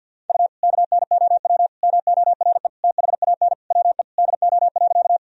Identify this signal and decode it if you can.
Morse code, heard as a keyed tone, about 47 wpm, 700 Hz. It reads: U KNOW MORE THAN WE DO.
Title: Sentences from Top 100 Words and Abbreviations 50wpm